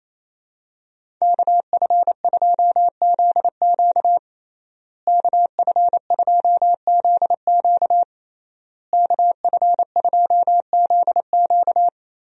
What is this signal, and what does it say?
Morse code, heard as a keyed tone, about 28 wpm, 700 Hz. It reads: KF2ZQ KF2ZQ KF2ZQ
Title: USA Call Signs - 5 Characters - Repeated 3x - 28wpm